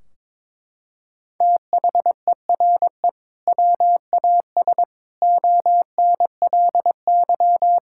Morse code, heard as a keyed tone, about 22 wpm, 700 Hz. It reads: THERE WAS ONLY
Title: Sentences from Top 1600 Words 22wpm